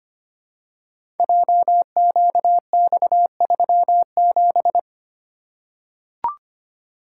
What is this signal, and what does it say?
Morse code, heard as a keyed tone, about 25 wpm, 700 Hz. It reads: JQX37
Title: Code Groups - 5 Letter:Numbers - 25wpm